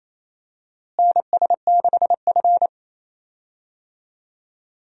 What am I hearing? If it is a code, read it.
Morse code, heard as a keyed tone, about 28 wpm, 700 Hz. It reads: NS6F
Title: USA Call Signs - 4 Characters - Repeated 3x - 28wpm